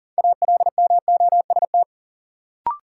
Morse code, heard as a keyed tone, about 40 wpm, 700 Hz. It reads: ALMOST
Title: Sentences from Top 700 Words - New Word Drill - 40wpm